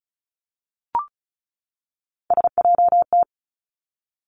Morse code, heard as a keyed tone, about 36 wpm, 700 Hz. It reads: SJT
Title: Three Challenging Letter-Number Combinations 35wpm